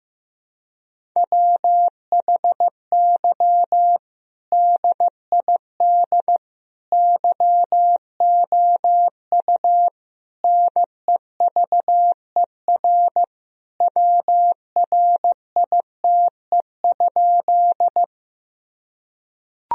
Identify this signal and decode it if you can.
Morse code, heard as a keyed tone, about 16 wpm, 700 Hz. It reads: WHY DID YOU NEVER WRITE?
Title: Sentences from Top 1000 Words - Review Part 1 - 15wpm